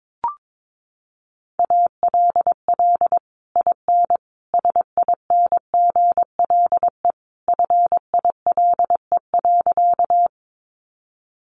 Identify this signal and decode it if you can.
Morse code, heard as a keyed tone, about 22 wpm, 700 Hz. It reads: ALL IN SINGLE FILE.
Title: Sentences from Top 1200 Words 22wpm